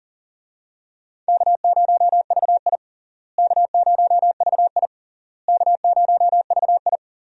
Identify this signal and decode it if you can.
Morse code, heard as a keyed tone, about 40 wpm, 700 Hz. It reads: K0VI K0VI K0VI
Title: USA Call Signs - 4 Characters - Repeated 3x - 40wpm